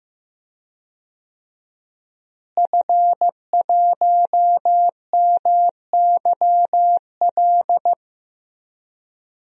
Morse code, heard as a keyed tone, about 15 wpm, 700 Hz. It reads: F1MYL
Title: DXCC-100 Call Signs - 5 Characters - Repeated 3x - 15wpm